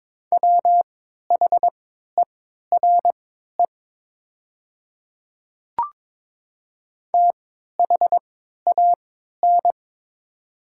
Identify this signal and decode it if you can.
Morse code, heard as a keyed tone, about 22 wpm, spaced as for 13 wpm, 700 Hz. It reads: WHERE THAN
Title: 3x Character Spacing - Top 200 Words - 22wpm